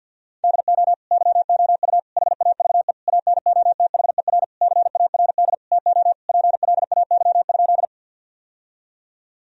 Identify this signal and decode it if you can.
Morse code, heard as a keyed tone, about 50 wpm, 700 Hz. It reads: DO YOU HAVE ANOTHER CARD TO PLAY?
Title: Sentences from Top 1000 Words - Review Part 4 - 50wpm